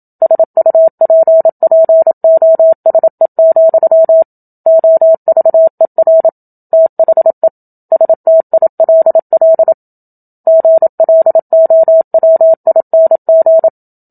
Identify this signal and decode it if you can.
Morse code, heard as a keyed tone, about 27 wpm, 650 Hz. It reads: SUPPOSE, OVER THE STILL GLOWING
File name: Book Two / 01-Chapter 1 - 27 WPM